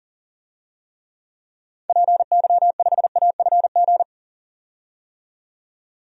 Morse code, heard as a keyed tone, about 40 wpm, 700 Hz. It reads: PY5AFG
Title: DXCC-100 Call Signs - 6 Characters - 40wpm